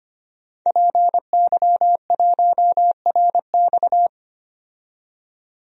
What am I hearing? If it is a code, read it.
Morse code, heard as a keyed tone, about 25 wpm, 700 Hz. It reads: PY1RX
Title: DXCC-100 Call Signs - 5 Characters - 25wpm